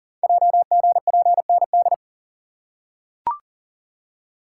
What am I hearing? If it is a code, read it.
Morse code, heard as a keyed tone, about 40 wpm, 700 Hz. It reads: JGPND
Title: Code Groups - 5 Letters - 40wpm